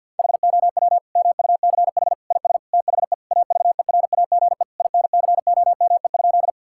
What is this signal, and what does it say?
Morse code, heard as a keyed tone, about 50 wpm, 700 Hz. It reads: HOW MUCH IS THE AVERAGE INCOME?
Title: Sentences from Top 1000 Words - Review Part 6 - 50wpm